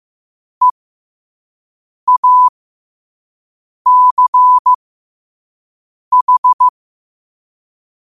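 Morse code, heard as a keyed tone, about 15 wpm, spaced as for 5 wpm, 1000 Hz. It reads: EACH
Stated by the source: Morse code quote of the day sent at 5 wpm.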